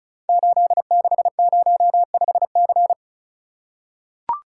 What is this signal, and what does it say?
Morse code, heard as a keyed tone, about 36 wpm, 700 Hz. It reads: 8605C
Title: Code Groups - 5 Letter:Numbers - 35wpm